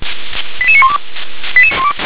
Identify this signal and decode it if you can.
Five-tone selective call (ZVEI), heard as quick runs of five tones, tones of about 70 ms. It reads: 88912
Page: SDR/ZVEI